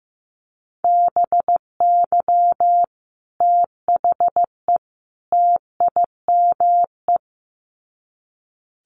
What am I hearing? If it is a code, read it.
Morse code, heard as a keyed tone, about 15 wpm, 700 Hz. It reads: BY THE TIME
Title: Sets of 3 Words 15wpm